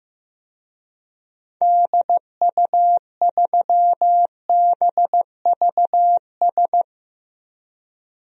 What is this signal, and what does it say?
Morse code, heard as a keyed tone, about 15 wpm, 700 Hz. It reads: DU3BVS
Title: DXCC-100 Call Signs - 6 Characters - 15wpm